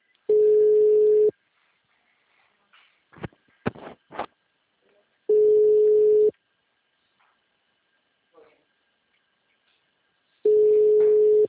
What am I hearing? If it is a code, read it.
Ringback tone (Europe, 425 Hz), with speech.
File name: out- -2028- - - .10758